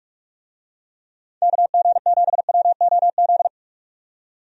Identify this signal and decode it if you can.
Morse code, heard as a keyed tone, about 46 wpm, 700 Hz. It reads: KG7WOZ
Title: USA Call Signs - 6 Characters - 45wpm